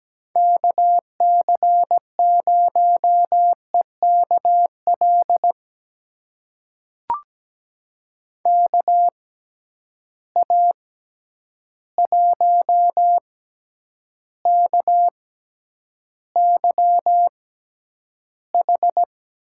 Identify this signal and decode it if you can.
Morse code, heard as a keyed tone, about 17 wpm, spaced as for 9 wpm, 700 Hz. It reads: KC0EKL KA1KYH
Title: USA Call Signs - 6 Characters - 6x Character Spacing - 17wpm